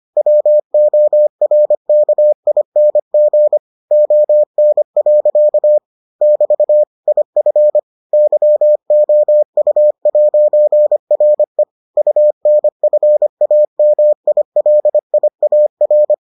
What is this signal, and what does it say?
Morse code, heard as a keyed tone, about 25 wpm, 600 Hz. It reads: WORKING ON. <BT> IF YOU'RE UNFAMILIAR